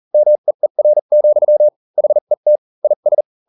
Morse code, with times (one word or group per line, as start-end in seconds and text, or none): 0.1-1.7 MEER,
2.0-2.6 HET
2.8-3.2 IS